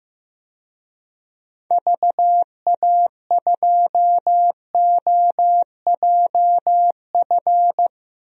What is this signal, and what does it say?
Morse code, heard as a keyed tone, about 15 wpm, 700 Hz. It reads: VA2OJF
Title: DXCC-100 Call Signs - 6 Characters - Repeated 3x - 15wpm